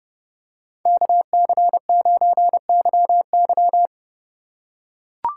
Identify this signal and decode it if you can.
Morse code, heard as a keyed tone, about 29 wpm, 700 Hz. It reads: KC9YY